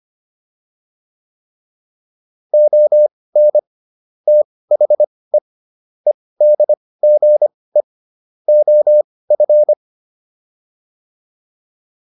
Morse code, heard as a keyed tone, about 25 wpm, spaced as for 18 wpm, 600 Hz. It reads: ON THE EDGE OF
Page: intermediate-practice-files